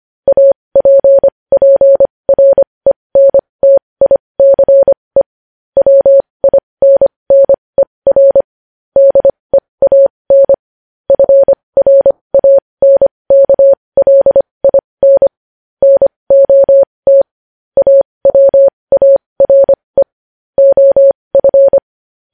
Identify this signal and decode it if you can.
Morse code, heard as a keyed tone, about 25 wpm, spaced as for 20 wpm, 550 Hz. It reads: APPRENTICE WINNER DEAN FRANKLIN NOT AWARE OF
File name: podcast / News headlines-1x-25x20